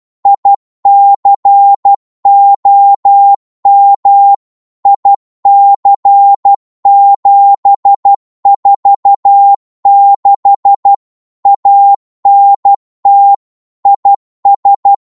Morse code, heard as a keyed tone, about 12 wpm, 800 Hz. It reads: ICOM IC746 ANT IS
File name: Morse / qso-12wpm